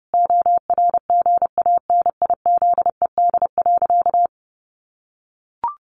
Morse code, heard as a keyed tone, about 30 wpm, 700 Hz. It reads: ORGANIZED.
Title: Sentences from Top 1300 Words 30wpm